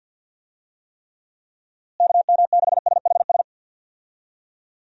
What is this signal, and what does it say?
Morse code, heard as a keyed tone, about 47 wpm, 700 Hz. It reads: KM6SHS
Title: USA Call Signs - 6 Characters - Repeated 2x - 50wpm